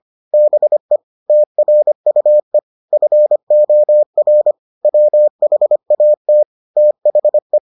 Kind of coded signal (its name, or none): Morse code